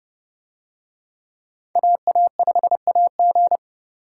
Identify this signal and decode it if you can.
Morse code, heard as a keyed tone, about 30 wpm, 700 Hz. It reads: AA5AG